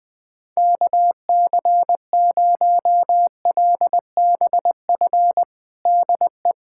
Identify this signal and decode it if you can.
Morse code, heard as a keyed tone, about 20 wpm, 700 Hz. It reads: KC0LBF DE
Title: USA Callsign DE Callsign - 6 Characters - 20wpm